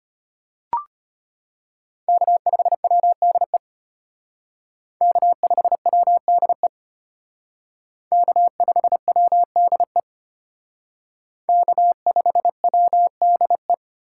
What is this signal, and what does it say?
Morse code, heard as a keyed tone, about 32 wpm, 700 Hz. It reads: K5WDE K5WDE K5WDE K5WDE